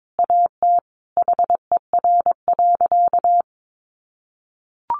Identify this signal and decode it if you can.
Morse code, heard as a keyed tone, about 22 wpm, 700 Hz. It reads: AT HER.
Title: Sentences from Top 200 Words 22wpm